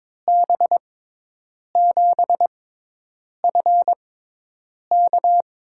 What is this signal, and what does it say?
Morse code, heard as a keyed tone, about 21 wpm, spaced as for 8 wpm, 700 Hz. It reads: B7FK
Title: USA Call Signs - 5 Characters - 6x Character Spacing - 22wpm